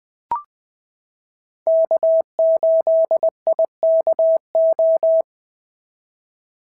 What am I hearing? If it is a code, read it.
Morse code, heard as a keyed tone, about 20 wpm, 650 Hz. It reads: K8IKO
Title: POTA QSO Practice - 20wpm